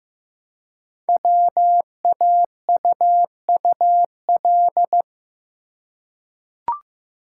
Morse code, heard as a keyed tone, about 15 wpm, 700 Hz. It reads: WAUUL
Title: Code Groups - 5 Letters - 15wpm